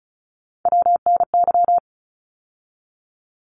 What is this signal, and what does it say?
Morse code, heard as a keyed tone, about 34 wpm, 700 Hz. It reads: WNY
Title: ARRL Sections 35wpm